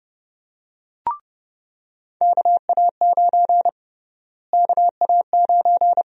Morse code, heard as a keyed tone, about 29 wpm, 700 Hz. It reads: KA9 KA9
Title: USA Call Sign Prefixes - Repeated 2x - 30wpm